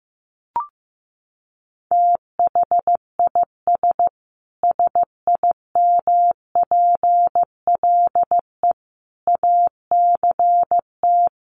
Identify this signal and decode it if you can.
Morse code, heard as a keyed tone, about 15 wpm, 700 Hz. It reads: THIS SIMPLE ACT
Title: Sentences from Top 1500 Words 15wpm